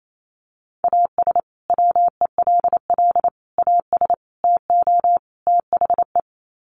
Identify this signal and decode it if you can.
Morse code, heard as a keyed tone, about 28 wpm, 700 Hz. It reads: AS WELL AS TO THE
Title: Sets of 5 Words Encore2 28wpm